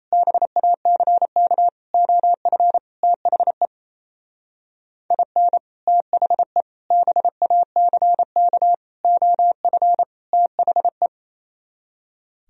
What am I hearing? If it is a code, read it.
Morse code, heard as a keyed tone, about 35 wpm, 700 Hz. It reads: BACK OF THE IN THE BACK OF THE